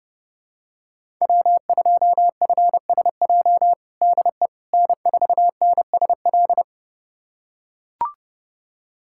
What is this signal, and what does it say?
Morse code, heard as a keyed tone, about 32 wpm, 700 Hz. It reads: W2FSJ DE N4NSL
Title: USA Callsign DE Callsign - 5 Characters - 30wpm